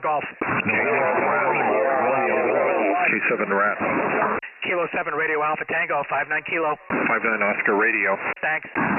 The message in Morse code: none